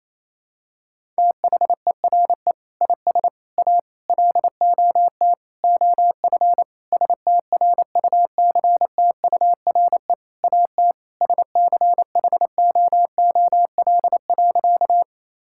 Morse code, heard as a keyed tone, about 28 wpm, 700 Hz. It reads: THERE IS A LOT OF STRUCTURE AT SCHOOL.
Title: Sentences from Top 1000 Words - Review Part 6 - 28wpm